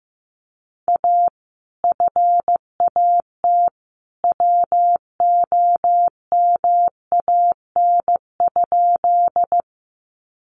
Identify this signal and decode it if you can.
Morse code, heard as a keyed tone, about 15 wpm, 700 Hz. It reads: A FAT WOMAN?
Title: Sentences from Top 1100 Words 15wpm